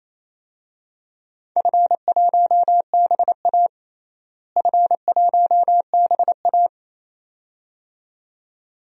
Morse code, heard as a keyed tone, about 28 wpm, 700 Hz. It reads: F1BA F1BA
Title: DXCC-100 Call Signs - 4 Characters - Repeated 2x - 28wpm